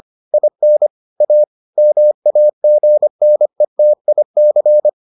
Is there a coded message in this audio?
Yes: Morse code from 0.3 to 4.9 s, heard as a keyed tone, about 25 wpm, 600 Hz. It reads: IN A MAGNETIC